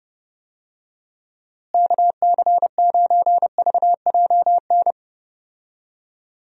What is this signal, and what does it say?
Morse code, heard as a keyed tone, about 30 wpm, 700 Hz. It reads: KC9VJN
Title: USA Call Signs - 6 Characters - Repeated 2x - 30wpm